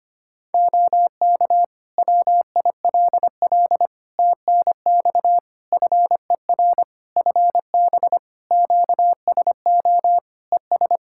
Morse code, heard as a keyed tone, about 25 wpm, 700 Hz. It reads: OK WILL TNX FER FB QSO ES
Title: Practice QSO - Exchange 4 - Ending - 25wpm